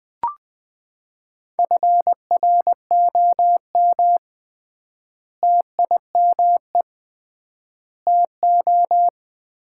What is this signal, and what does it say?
Morse code, heard as a keyed tone, about 21 wpm, 700 Hz. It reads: FROM TIME TO
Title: 3x Word-Spacing - Sets of 4 Words - 20wpm